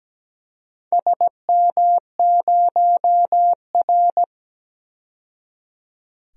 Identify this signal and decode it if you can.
Morse code, heard as a keyed tone, about 17 wpm, 700 Hz. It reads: SM0R
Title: DXCC-100 Call Signs - 4 Characters - 17wpm